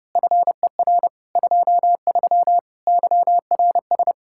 Morse code, heard as a keyed tone, about 26 wpm, 700 Hz. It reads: FER 23 YRS
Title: Speed-Racing Common QSO Phrases - 22wpm